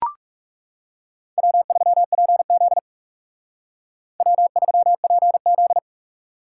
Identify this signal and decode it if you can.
Morse code, heard as a keyed tone, about 39 wpm, 700 Hz. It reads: W3PZ W3PZ